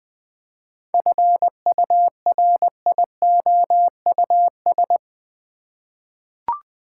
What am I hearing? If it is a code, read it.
Morse code, heard as a keyed tone, about 20 wpm, 700 Hz. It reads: FURIOUS
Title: 7 Letter Words - First 3 Letters Given - 20wpm